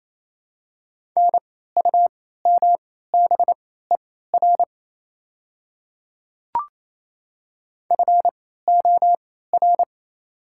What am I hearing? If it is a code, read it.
Morse code, heard as a keyed tone, about 27 wpm, spaced as for 16 wpm, 700 Hz. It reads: NUMBER FOR
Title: 3x Character Spacing - Top 300 Words - 28wpm